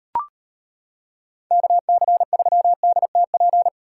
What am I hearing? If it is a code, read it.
Morse code, heard as a keyed tone, about 32 wpm, 700 Hz. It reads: KC3DTP